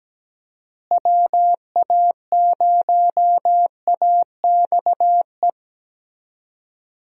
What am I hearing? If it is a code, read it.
Morse code, heard as a keyed tone, about 17 wpm, 700 Hz. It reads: WA0AXE
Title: USA Call Signs - 6 Characters - Repeated 2x - 17wpm